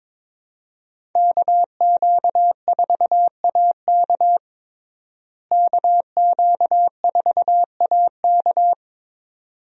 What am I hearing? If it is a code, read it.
Morse code, heard as a keyed tone, about 21 wpm, 700 Hz. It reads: KQ4AK KQ4AK
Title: USA Call Signs - 5 Characters - Repeated 2x - 22wpm